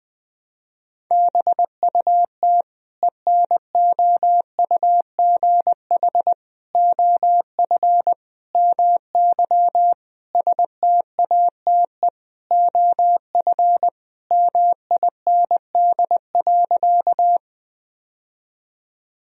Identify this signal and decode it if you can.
Morse code, heard as a keyed tone, about 20 wpm, 700 Hz. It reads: BUT ENOUGH OF MY STATE OF MIND.
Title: Sentences from Top 1000 Words - Review Part 2 - 20wpm